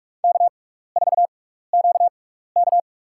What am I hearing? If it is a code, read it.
Morse code, heard as a keyed tone, about 46 wpm, spaced as for 16 wpm, 700 Hz. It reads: K4QK
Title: USA Call Signs - 6x Character Spacing - 45wpm